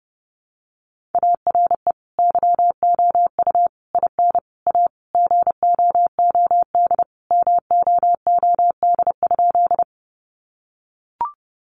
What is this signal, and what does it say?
Morse code, heard as a keyed tone, about 30 wpm, 700 Hz. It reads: ARE YOU IN A GOOD MOOD?
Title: Sentences from Top 1100 Words 30wpm